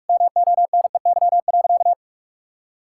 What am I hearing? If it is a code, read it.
Morse code, heard as a keyed tone, about 46 wpm, 700 Hz. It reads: MONEY.
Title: Sentences from Top 1000 Words - Review Part 5 - 45wpm